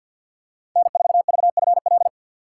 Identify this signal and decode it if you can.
Morse code, heard as a keyed tone, about 53 wpm, 700 Hz. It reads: N4VFL
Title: USA Call Signs - 5 Characters - 2x Character Spacing - 50wpm